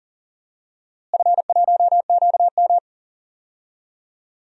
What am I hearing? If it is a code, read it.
Morse code, heard as a keyed tone, about 40 wpm, 700 Hz. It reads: F1QM
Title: DXCC-100 Call Signs - 4 Characters - Repeated 2x - 40wpm